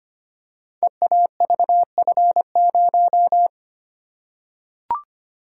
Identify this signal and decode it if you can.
Morse code, heard as a keyed tone, about 25 wpm, 700 Hz. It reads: EAVF0
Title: Code Groups - 5 Letter:Numbers - 25wpm